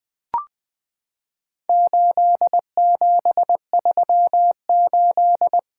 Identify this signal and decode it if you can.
Morse code, heard as a keyed tone, about 20 wpm, 700 Hz. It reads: 8738
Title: Four Letter-Number Combinations - 20wpm